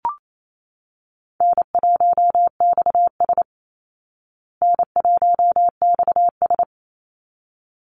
Morse code, heard as a keyed tone, about 28 wpm, 700 Hz. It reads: N1XS N1XS
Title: Easy Call Signs Repeated 2x - 28wpm